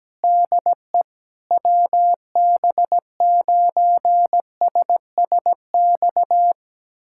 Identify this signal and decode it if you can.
Morse code, heard as a keyed tone, about 17 wpm, 700 Hz. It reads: DE WB9SSX
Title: USA Callsign DE Callsign - 6 Characters - 17wpm